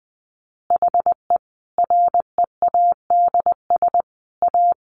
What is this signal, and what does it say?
Morse code, heard as a keyed tone, about 20 wpm, 700 Hz. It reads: HE READS A
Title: Additional Easy Phrases 20wpm